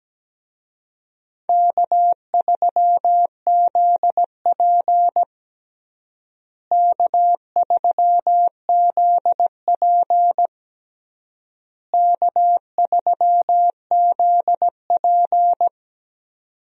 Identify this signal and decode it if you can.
Morse code, heard as a keyed tone, about 17 wpm, 700 Hz. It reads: K3ZP K3ZP K3ZP
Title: USA Call Signs - 4 Characters - Repeated 3x - 17wpm